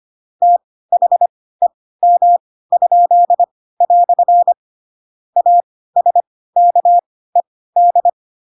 Morse code, heard as a keyed tone, about 25 wpm, spaced as for 16 wpm, 700 Hz. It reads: THEM?" ASKED
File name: Pinocchio / Pinocchio ch7 10 25 ik1hge